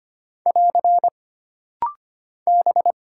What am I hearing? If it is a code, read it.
Morse code, heard as a keyed tone, about 25 wpm, 700 Hz. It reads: <AR> B
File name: Single Character (Letter:Number:Punctuation:Prosign) - Rapid-Fire - 25wpm